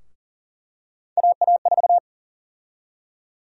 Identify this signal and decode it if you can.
Morse code, heard as a keyed tone, about 40 wpm, 700 Hz. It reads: AA4
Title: USA Call Sign Prefixes - Repeated 2x - 40wpm